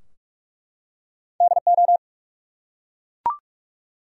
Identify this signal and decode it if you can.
Morse code, heard as a keyed tone, about 43 wpm, 700 Hz. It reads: DO
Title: Top 100 Words 45wpm